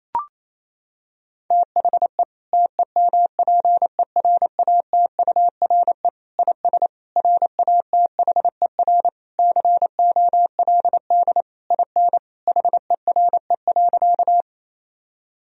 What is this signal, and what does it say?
Morse code, heard as a keyed tone, about 27 wpm, 700 Hz. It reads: THE TEMPERATURE IS RATHER COLD IN HERE.
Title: Sentences from Top 1000 Words - Review Part 4 - 28wpm